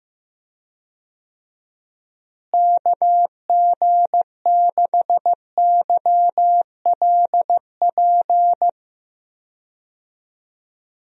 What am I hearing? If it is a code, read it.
Morse code, heard as a keyed tone, about 15 wpm, 700 Hz. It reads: KG6YLP
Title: USA Call Signs - 6 Characters - Repeated 3x - 15wpm